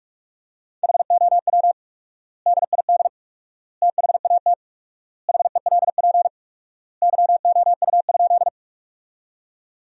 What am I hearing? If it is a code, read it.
Morse code, heard as a keyed tone, about 46 wpm, 700 Hz. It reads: HOW DID THAT HELP YOU?
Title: 4x Word-Spacing - Sentences from Top 200 Words - 45wpm